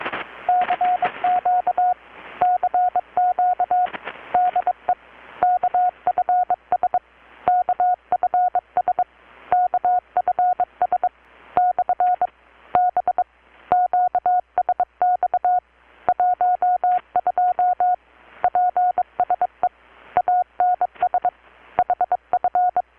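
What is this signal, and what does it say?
Morse code, heard as a keyed tone, about 22 wpm, 720 Hz. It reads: CQ CQ DE KFS KFS KFS / B QSX 12 PSE ANS HF
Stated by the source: Morse from KPH's Night of Nights XX maritime radiotelegraphy event, received in Chiapas, Mexico about 3,800 km away on a Kenwood R-600 with a grounded 10 m long-wire antenna.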